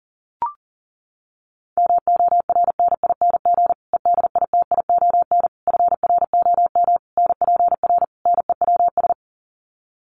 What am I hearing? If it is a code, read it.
Morse code, heard as a keyed tone, about 40 wpm, 700 Hz. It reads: MORNING EDITION FROM NPR NEWS
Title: Sets of 5 Words Encore 40wpm